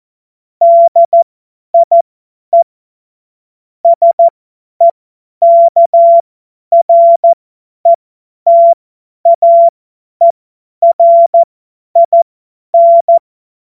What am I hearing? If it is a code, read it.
Morse code, heard as a keyed tone, about 14 wpm, spaced as for 10 wpm, 680 Hz. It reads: DIE SEKRETAERIN
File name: Morsekurs / Witze / Witz 23. Mai 2023 50 F70